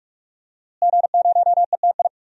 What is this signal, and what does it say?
Morse code, heard as a keyed tone, about 42 wpm, 700 Hz. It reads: G0ETI
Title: DXCC-100 Call Signs - 5 Characters - Repeated 3x - 45wpm